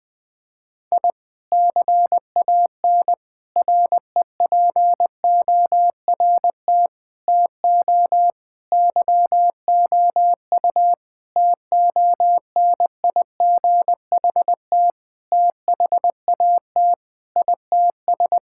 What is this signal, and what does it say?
Morse code, heard as a keyed tone, about 21 wpm, 700 Hz. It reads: I CAN REPORT TO YOU TONIGHT THAT ITS
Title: Sentences from Top 1000 Words - Review Part 6 - 20wpm